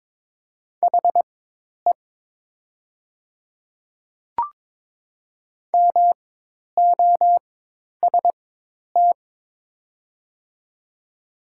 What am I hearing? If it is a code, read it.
Morse code, heard as a keyed tone, about 22 wpm, spaced as for 10 wpm, 700 Hz. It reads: HE MOST
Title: 4x Character Spacing - Top 100 Words - 22wpm